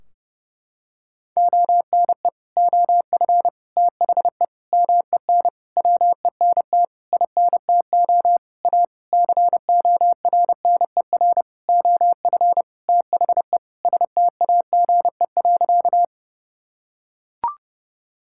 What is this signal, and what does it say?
Morse code, heard as a keyed tone, about 30 wpm, 700 Hz. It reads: ONE OF THE MEN WENT INTO A CORNER OF THE STAGE.
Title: Sentences from Top 1600 Words 30wpm